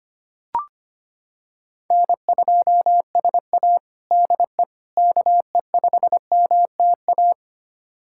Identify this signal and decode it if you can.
Morse code, heard as a keyed tone, about 25 wpm, 700 Hz. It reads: N2SA DE KE5MTA
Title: USA Callsign DE Callsign 25wpm